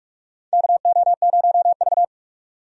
Morse code, heard as a keyed tone, about 46 wpm, 700 Hz. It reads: KO0V
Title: USA Call Signs - 4 Characters - 45wpm